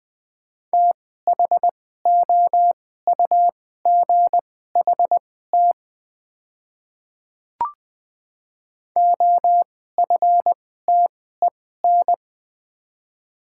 Morse code, heard as a keyed tone, about 20 wpm, spaced as for 14 wpm, 700 Hz. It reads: THOUGHT OFTEN
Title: 2x Character Spacing - Top 300 Words - 20wpm